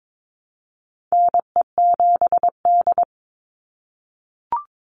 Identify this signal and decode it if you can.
Morse code, heard as a keyed tone, about 22 wpm, 700 Hz. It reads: NE7D
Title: Call Signs - 22wpm